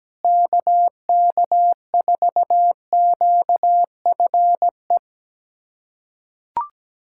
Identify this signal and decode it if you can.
Morse code, heard as a keyed tone, about 17 wpm, 700 Hz. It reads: KK4QFE
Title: USA Callsign DE Callsign - 6 Characters - 17wpm